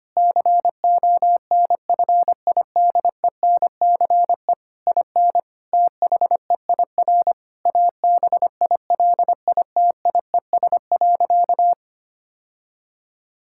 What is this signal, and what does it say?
Morse code, heard as a keyed tone, about 25 wpm, 700 Hz. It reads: CONFIDENCE IN THEIR ABILITIES.
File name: practice / 25wpm / Difficult Sentences with Hints - 25wpm